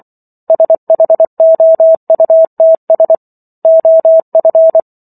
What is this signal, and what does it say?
Morse code, heard as a keyed tone, about 24 wpm, 650 Hz. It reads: SHOUTS OF